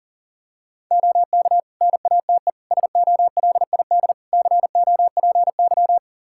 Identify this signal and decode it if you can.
Morse code, heard as a keyed tone, about 40 wpm, 700 Hz. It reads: OK NATE SOLID COPY
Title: Practice QSO - Exchange 3 - More Info - 40wpm